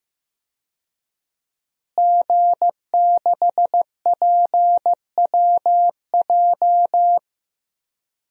Morse code, heard as a keyed tone, about 15 wpm, 700 Hz. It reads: G6PWJ
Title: DXCC-100 Call Signs - 5 Characters - Repeated 3x - 15wpm